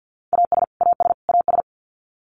Morse code, heard as a keyed tone, about 50 wpm, 700 Hz. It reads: AS AS AS
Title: Top 200 Words Repeated 50wpm